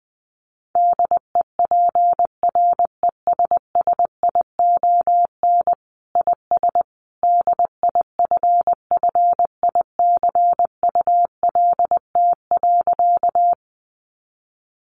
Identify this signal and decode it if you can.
Morse code, heard as a keyed tone, about 19 wpm, 700 Hz. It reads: DEPRESSION IS DIFFICULT.
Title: Sentences from Top 1300 Words 20wpm